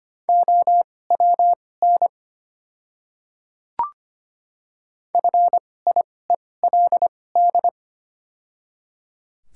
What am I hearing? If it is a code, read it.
Morse code, heard as a keyed tone, about 25 wpm, spaced as for 18 wpm, 700 Hz. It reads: OWN FIELD
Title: 2x Character Spacing - Top 500 Words - 25wpm